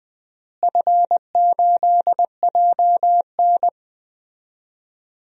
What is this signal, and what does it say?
Morse code, heard as a keyed tone, about 20 wpm, 700 Hz. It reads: F8JN
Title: DXCC-100 Call Signs - 4 Characters - Repeated 3x - 20wpm